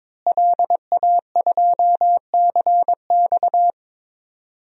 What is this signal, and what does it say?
Morse code, heard as a keyed tone, about 22 wpm, 700 Hz. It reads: LA2CX
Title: DXCC-100 Call Signs - 5 Characters - Repeated 2x - 22wpm